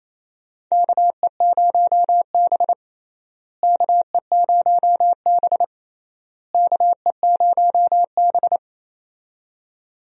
Morse code, heard as a keyed tone, about 28 wpm, 700 Hz. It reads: KE0B KE0B KE0B
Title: USA Call Signs - 4 Characters - Repeated 3x - 28wpm